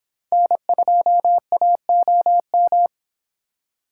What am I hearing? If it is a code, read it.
Morse code, heard as a keyed tone, about 26 wpm, 700 Hz. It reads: N2AOM